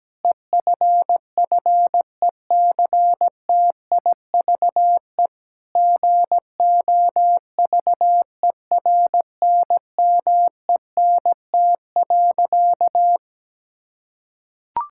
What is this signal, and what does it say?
Morse code, heard as a keyed tone, about 17 wpm, 700 Hz. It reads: EFFECTIVE GOVERNMENT.
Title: Sentences from Top 1000 Words - Review Part 6 - 17wpm